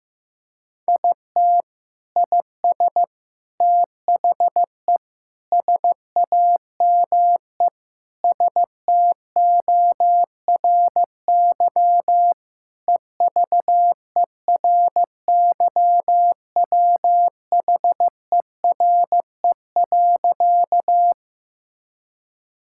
Morse code, heard as a keyed tone, about 15 wpm, 700 Hz. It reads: IT IS THE SAME STORY EVERYWHERE.
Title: Sentences from Top 1600 Words 15wpm